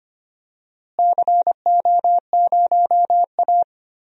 Morse code, heard as a keyed tone, about 25 wpm, 700 Hz. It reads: CO0A
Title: DXCC-100 Call Signs - 4 Characters - Repeated 2x - 25wpm